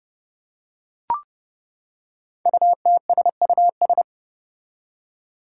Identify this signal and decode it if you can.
Morse code, heard as a keyed tone, about 30 wpm, 700 Hz. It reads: UTSUS
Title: Five Challenging Letter-Number Combinations 30wpm